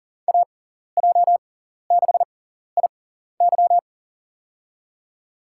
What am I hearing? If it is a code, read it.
Morse code, heard as a keyed tone, about 40 wpm, spaced as for 14 wpm, 700 Hz. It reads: AJ6IY
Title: USA Call Signs - 6x Character Spacing - 40wpm